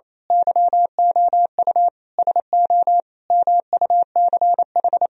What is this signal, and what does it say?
Morse code, heard as a keyed tone, about 26 wpm, 700 Hz. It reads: YOU SO MUCH
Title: Speed-Racing - Sets of 5 Words 28wpm